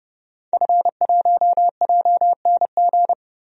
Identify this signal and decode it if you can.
Morse code, heard as a keyed tone, about 30 wpm, 700 Hz. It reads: F1JNG
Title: DXCC-100 Call Signs - 5 Characters - 30wpm